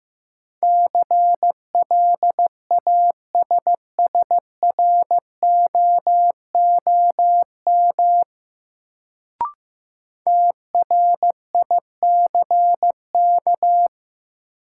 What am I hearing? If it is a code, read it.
Morse code, heard as a keyed tone, about 15 wpm, 700 Hz. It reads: CLASSROOM TRICK
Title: Sentences from Top 1500 Words - New Word Drill - 15wpm